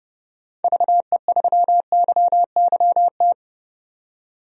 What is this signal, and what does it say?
Morse code, heard as a keyed tone, about 32 wpm, 700 Hz. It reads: VE3YYT
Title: DXCC-100 Call Signs - 6 Characters - 30wpm